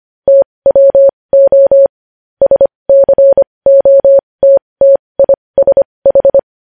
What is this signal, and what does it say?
Morse code, heard as a keyed tone, about 25 wpm, spaced as for 20 wpm, 550 Hz. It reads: TWO SCOTTISH
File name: podcast / News headlines-1x-25x20